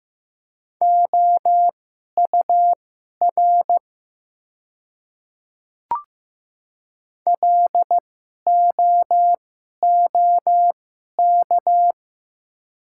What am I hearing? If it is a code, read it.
Morse code, heard as a keyed tone, about 15 wpm, spaced as for 11 wpm, 700 Hz. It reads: OUR LOOK
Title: 2x Character Spacing - Top 200 Words - 15wpm